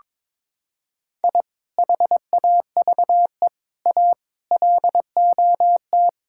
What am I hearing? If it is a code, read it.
Morse code, heard as a keyed tone, about 22 wpm, 700 Hz. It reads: I HAVE A LOT
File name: practice / 22wpm / Sentences with 2-Word Phrases - 22wpm